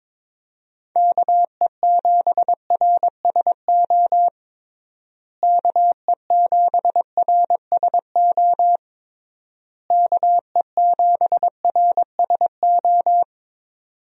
Morse code, heard as a keyed tone, about 22 wpm, 700 Hz. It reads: KE7RSO KE7RSO KE7RSO
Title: USA Call Signs - 6 Characters - Repeated 3x - 22wpm